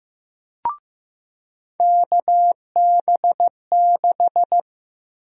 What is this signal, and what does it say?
Morse code, heard as a keyed tone, about 15 wpm, 700 Hz. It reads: KB6
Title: USA Call Sign Prefixes - 15wpm